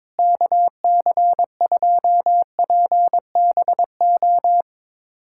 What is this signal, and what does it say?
Morse code, heard as a keyed tone, about 22 wpm, 700 Hz. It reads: KC2PBO
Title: USA Callsign DE Callsign 22wpm